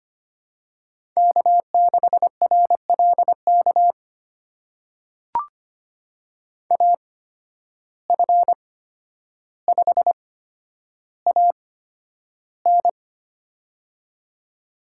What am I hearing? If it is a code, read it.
Morse code, heard as a keyed tone, about 25 wpm, spaced as for 11 wpm, 700 Hz. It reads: K6RLK AF5AN
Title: USA Call Signs - 5 Characters - 8x Character Spacing - 25wpm